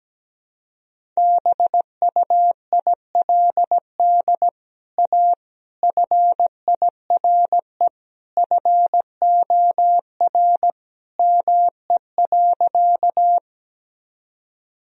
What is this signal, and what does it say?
Morse code, heard as a keyed tone, about 17 wpm, 700 Hz. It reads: BUILD A FIRE FOR ME.
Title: Sentences from Top 1000 Words - Review Part 1 - 17wpm